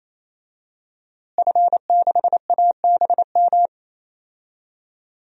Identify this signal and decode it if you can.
Morse code, heard as a keyed tone, about 27 wpm, 700 Hz. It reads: F6ABM
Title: DXCC-100 Call Signs - 5 Characters - Repeated 3x - 28wpm